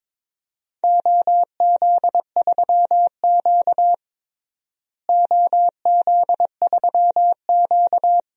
Morse code, heard as a keyed tone, about 22 wpm, 700 Hz. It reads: OZ3Q OZ3Q
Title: DXCC-100 Call Signs - 4 Characters - Repeated 3x - 22wpm